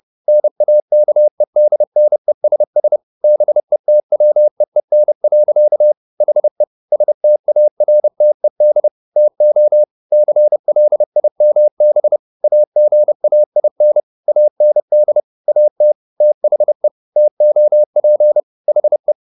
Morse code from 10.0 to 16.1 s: CLIMB AGAIN AND AT

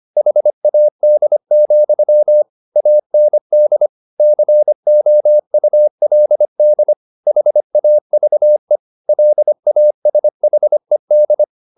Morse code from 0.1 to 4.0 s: HAD, AND

